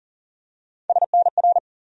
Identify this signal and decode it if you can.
Morse code, heard as a keyed tone, about 40 wpm, 700 Hz. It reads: SNR
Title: USA Call Sign Postfixes - 3 Letters 40wpm